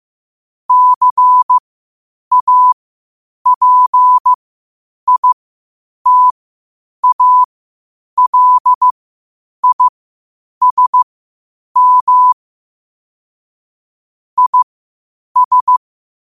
Morse code, heard as a keyed tone, about 15 wpm, spaced as for 8 wpm, 1000 Hz. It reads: CAPITALISM IS
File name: qotdfiles / 8wpm